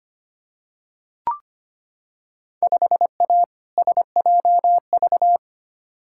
Morse code, heard as a keyed tone, about 25 wpm, 700 Hz. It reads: 5A SJV
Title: Field Day Exchanges - 25wpm